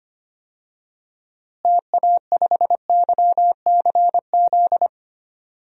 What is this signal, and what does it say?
Morse code, heard as a keyed tone, about 25 wpm, 700 Hz. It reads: TA5YCZ
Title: DXCC-100 Call Signs - 6 Characters - Repeated 3x - 25wpm